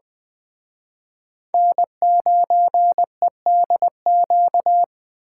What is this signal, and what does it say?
Morse code, heard as a keyed tone, about 23 wpm, 700 Hz. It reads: N9EDQ
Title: Speed Racing - USA 1x3 Call Signs 20wpm